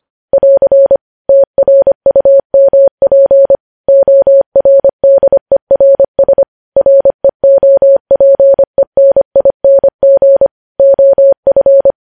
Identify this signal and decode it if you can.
Morse code, heard as a keyed tone, about 25 wpm, 550 Hz. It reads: <AR> TRUMP ORDERS REOPENING OF
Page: news-headlines-in-morse-code-at-25-wpm